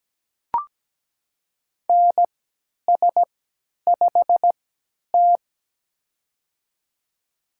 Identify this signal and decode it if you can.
Morse code, heard as a keyed tone, about 17 wpm, spaced as for 10 wpm, 700 Hz. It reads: NS5T
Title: USA Call Signs - 4 Characters - 3x Character Spacing - 17wpm